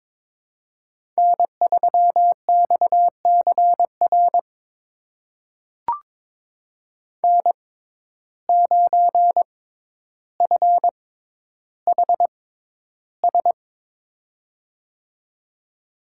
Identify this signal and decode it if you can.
Morse code, heard as a keyed tone, about 22 wpm, spaced as for 11 wpm, 700 Hz. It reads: N3XCR N9FHS
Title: USA Call Signs - 5 Characters - 6x Character Spacing - 22wpm